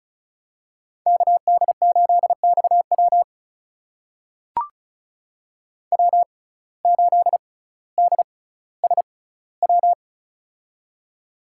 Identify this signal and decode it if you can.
Morse code, heard as a keyed tone, about 34 wpm, spaced as for 18 wpm, 700 Hz. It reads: KD8XW W8DSW
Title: USA Call Signs - 5 Characters - 6x Character Spacing - 35wpm